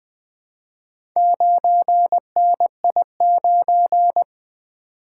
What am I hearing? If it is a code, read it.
Morse code, heard as a keyed tone, about 20 wpm, 700 Hz. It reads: 9NI9
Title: Four Letter-Number Combinations - 20wpm